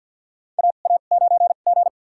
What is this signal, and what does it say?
Morse code, heard as a keyed tone, about 50 wpm, spaced as for 36 wpm, 690 Hz. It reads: AA9G
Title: USA Call Signs - 2x Character Spacing - 50wpm